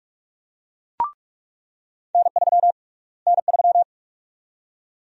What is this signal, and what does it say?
Morse code, heard as a keyed tone, about 47 wpm, 700 Hz. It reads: N3 N3
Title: USA Call Sign Prefixes - Repeated 2x - 45wpm